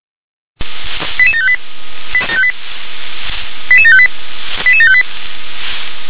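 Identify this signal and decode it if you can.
Five-tone selective call (ZVEI), heard as quick runs of five tones, tones of about 70 ms. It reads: 88758 88758 88758
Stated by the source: ZVEI five-tone selective call received by SDR.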